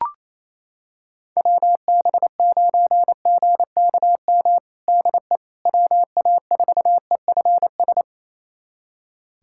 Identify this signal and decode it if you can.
Morse code, heard as a keyed tone, about 28 wpm, 700 Hz. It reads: WB9GKM DE WA4EFS
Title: USA Callsign DE Callsign - 6 Characters - 28wpm